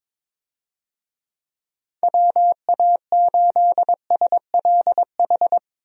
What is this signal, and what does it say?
Morse code, heard as a keyed tone, about 22 wpm, 700 Hz. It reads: WA8SLH
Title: USA Call Signs - 6 Characters - Repeated 3x - 22wpm